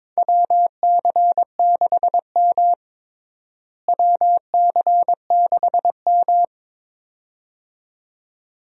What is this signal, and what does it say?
Morse code, heard as a keyed tone, about 22 wpm, 700 Hz. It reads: WC6M WC6M
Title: USA Call Signs - 4 Characters - Repeated 2x - 22wpm